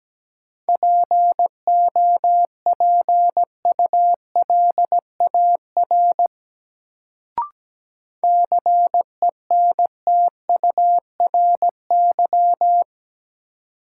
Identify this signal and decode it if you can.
Morse code, heard as a keyed tone, about 17 wpm, 700 Hz. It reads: POPULAR CENTURY
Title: Sets of 5 Words - New Word Drill - 17wpm